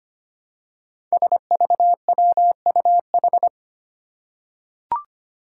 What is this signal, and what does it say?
Morse code, heard as a keyed tone, about 25 wpm, 700 Hz. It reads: SVWUH
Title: Five Challenging Letter-Number Combinations 25wpm